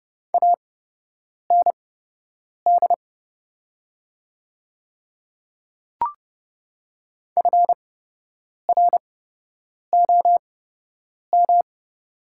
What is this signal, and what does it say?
Morse code, heard as a keyed tone, about 31 wpm, spaced as for 8 wpm, 700 Hz. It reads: AND FROM
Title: 8x Character Spacing - Top 100 Words - 30wpm